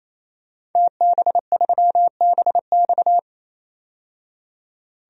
Morse code, heard as a keyed tone, about 28 wpm, 700 Hz. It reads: TB3BX
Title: DXCC-100 Call Signs - 5 Characters - 28wpm